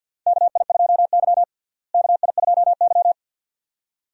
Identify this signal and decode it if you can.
Morse code, heard as a keyed tone, about 52 wpm, 700 Hz. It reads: KI2Y KI2Y
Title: USA Call Signs - 4 Characters - Repeated 2x - 50wpm